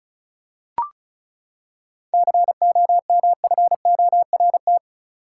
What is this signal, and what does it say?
Morse code, heard as a keyed tone, about 36 wpm, 700 Hz. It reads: COMFORT
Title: Seven Letter Words - 35wpm